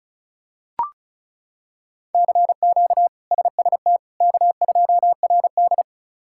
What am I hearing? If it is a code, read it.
Morse code, heard as a keyed tone, about 36 wpm, 700 Hz. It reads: CQ SST K2RD
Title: K1USN SST QSO Practice - 35wpm